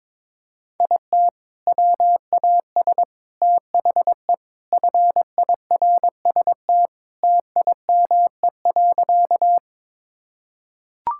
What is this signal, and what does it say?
Morse code, heard as a keyed tone, about 22 wpm, 700 Hz. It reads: IT WAS THE FIRST TIME.
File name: practice / 22wpm / Sentences with 4-Word Phrases - 22wpm